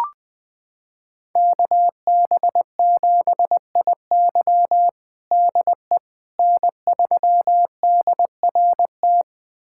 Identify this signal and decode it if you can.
Morse code, heard as a keyed tone, about 19 wpm, 700 Hz. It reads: KB7IY DE N3DRT
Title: USA Callsign DE Callsign - 5 Characters - 20wpm